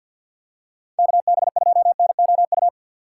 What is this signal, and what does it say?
Morse code, heard as a keyed tone, about 48 wpm, 700 Hz. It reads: KB2NOU
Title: USA Call Signs - 6 Characters - 4x Character Spacing - 50wpm